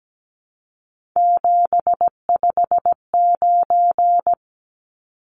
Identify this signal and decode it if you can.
Morse code, heard as a keyed tone, about 17 wpm, 700 Hz. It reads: 759
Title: Numbers - 3-Digit 17wpm